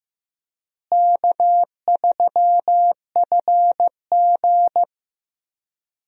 Morse code, heard as a keyed tone, about 15 wpm, 700 Hz. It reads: K3FG
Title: USA Call Signs - 4 Characters - Repeated 2x - 15wpm